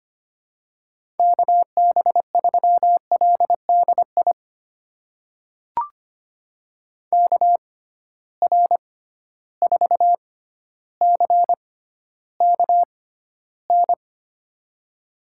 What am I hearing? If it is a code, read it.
Morse code, heard as a keyed tone, about 25 wpm, spaced as for 13 wpm, 700 Hz. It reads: KB3LDI KR4CKN
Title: USA Call Signs - 6 Characters - 6x Character Spacing - 25wpm